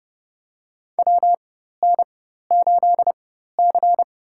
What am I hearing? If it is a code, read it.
Morse code, heard as a keyed tone, about 30 wpm, spaced as for 14 wpm, 700 Hz. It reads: WN8C
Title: USA Call Signs - 4 Characters - 4x Character Spacing - 30wpm